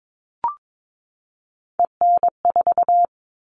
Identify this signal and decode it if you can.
Morse code, heard as a keyed tone, about 22 wpm, 700 Hz. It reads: EN4
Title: All Cut Numbers - 3-Digit 22wpm